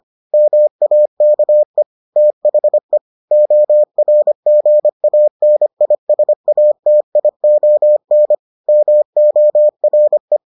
Morse code, heard as a keyed tone, about 25 wpm, 600 Hz. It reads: MAKE THE ORGANISATION MORE